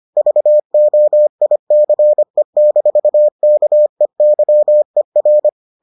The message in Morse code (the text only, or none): VOICE-KEYER